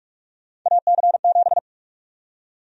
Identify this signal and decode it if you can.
Morse code, heard as a keyed tone, about 42 wpm, 700 Hz. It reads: AC7
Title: USA Call Sign Prefixes - 45wpm